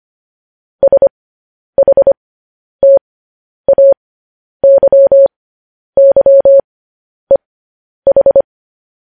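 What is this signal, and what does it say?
Morse code, heard as a keyed tone, about 25 wpm, spaced as for 10 wpm, 550 Hz. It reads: SHTAYYEH
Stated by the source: Morse code practice sending news headlines at 10 wpm.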